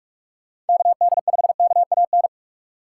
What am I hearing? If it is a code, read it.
Morse code, heard as a keyed tone, about 58 wpm, 700 Hz. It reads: KD5KAN